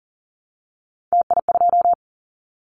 Morse code, heard as a keyed tone, about 37 wpm, 700 Hz. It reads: TI2
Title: Callsign Prefixes for Top 100 DXCC - 40wpm